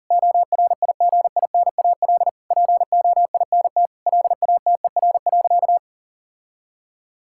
Morse code, heard as a keyed tone, about 40 wpm, 700 Hz. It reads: ORIGINAL POINT LATER.
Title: Sentences from Top 1000 Words - Review Part 6 - 40wpm